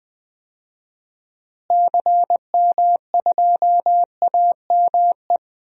Morse code, heard as a keyed tone, about 20 wpm, 700 Hz. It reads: CM2AME
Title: DXCC-100 Call Signs - 6 Characters - Repeated 2x - 20wpm